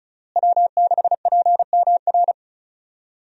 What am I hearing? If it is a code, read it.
Morse code, heard as a keyed tone, about 36 wpm, 700 Hz. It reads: W6PMR
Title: USA Call Signs - 5 Characters - Repeated 3x - 35wpm